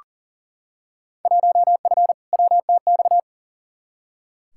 Morse code, heard as a keyed tone, about 40 wpm, 700 Hz. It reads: 1F WTX
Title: Field Day Exchanges Encore - 40wpm